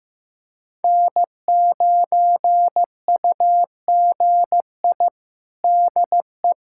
Morse code, heard as a keyed tone, about 16 wpm, 700 Hz. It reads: N9UGI DE
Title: USA Callsign DE Callsign - 5 Characters - 15wpm